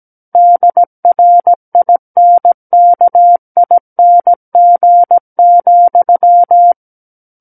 Morse code, heard as a keyed tone, about 17 wpm, 700 Hz. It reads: DRINKING,
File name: Book One / 08-Chapter 8 - 17 WPM